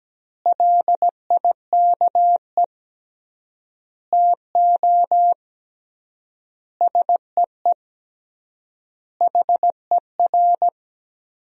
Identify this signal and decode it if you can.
Morse code, heard as a keyed tone, about 17 wpm, 700 Hz. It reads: LIKE TO SEE HER
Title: 3x Word-Spacing - Sentences from Top 200 Words - 17wpm